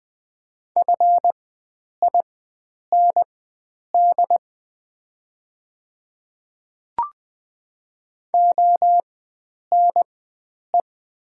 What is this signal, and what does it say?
Morse code, heard as a keyed tone, about 20 wpm, spaced as for 9 wpm, 700 Hz. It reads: FIND ONE
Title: 4x Character Spacing - Top 100 Words - 20wpm